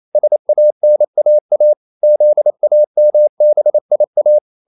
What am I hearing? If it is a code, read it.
Morse code, heard as a keyed tone, about 28 wpm, 600 Hz. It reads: SANAA ZAMBIA
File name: CW / Practice / NationalCapitals28